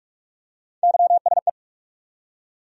Morse code, heard as a keyed tone, about 46 wpm, 700 Hz. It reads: YSE
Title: USA Call Sign Postfixes - 3 Letters 45wpm